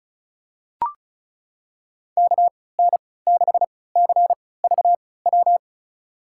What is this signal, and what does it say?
Morse code, heard as a keyed tone, about 36 wpm, spaced as for 20 wpm, 690 Hz. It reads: KN6CVW
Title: USA Call Signs - 6 Characters - 3x Character Spacing - 35wpm